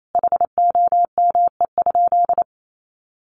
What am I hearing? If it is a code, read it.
Morse code, heard as a keyed tone, about 28 wpm, 700 Hz. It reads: HOME?
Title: Sentences from Top 1200 Words 28wpm